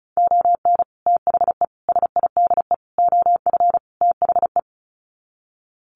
Morse code, heard as a keyed tone, about 34 wpm, 700 Hz. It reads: ON THE SIDE OF THE
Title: Sets of 5 Words 35wpm